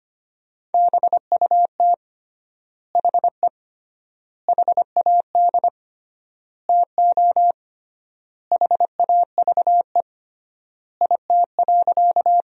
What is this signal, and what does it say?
Morse code, heard as a keyed tone, about 25 wpm, 700 Hz. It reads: BUT HE HAD TO HAVE IT.
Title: 3x Word-Spacing - Sentences from Top 100 Words - 25wpm